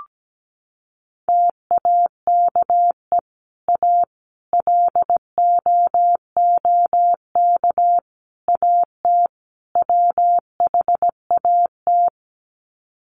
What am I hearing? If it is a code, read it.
Morse code, heard as a keyed tone, about 17 wpm, 700 Hz. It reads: TAKE A LOOK AT WHAT
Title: Sets of 5 Words Encore2 17wpm